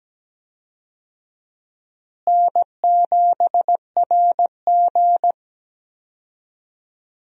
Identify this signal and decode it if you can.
Morse code, heard as a keyed tone, about 17 wpm, 700 Hz. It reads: N7RG
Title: USA Call Signs - 4 Characters - Repeated 2x - 17wpm